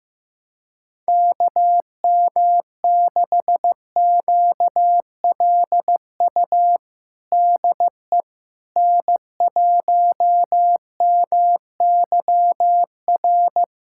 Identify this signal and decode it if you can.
Morse code, heard as a keyed tone, about 15 wpm, 700 Hz. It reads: KM6QLU DE N1MYR
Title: USA Callsign DE Callsign 15wpm